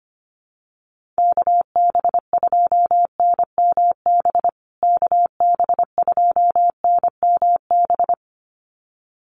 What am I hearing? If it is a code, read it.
Morse code, heard as a keyed tone, about 25 wpm, 700 Hz. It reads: KB2NMB KB2NMB
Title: USA Call Signs Repeated 2x 25wpm